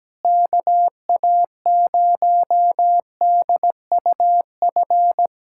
Morse code, heard as a keyed tone, about 17 wpm, 700 Hz. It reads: KA0DUF
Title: USA Callsign DE Callsign 17wpm